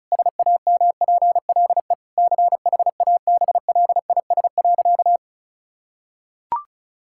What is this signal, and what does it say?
Morse code, heard as a keyed tone, about 34 wpm, 700 Hz. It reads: SAMPLE CHABLIS.
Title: Pangram Challenge v2 - 35wpm